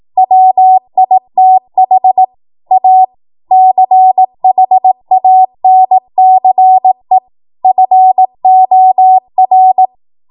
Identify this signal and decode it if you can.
Morse code, heard as a keyed tone, about 18 wpm, 750 Hz. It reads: WITH A CHANCE FOR
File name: Morse / 18WPM